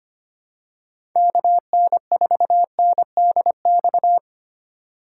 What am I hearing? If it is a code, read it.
Morse code, heard as a keyed tone, about 25 wpm, 700 Hz. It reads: KN4NDX
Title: USA Call Signs - 6 Characters - 4x Character Spacing - 25wpm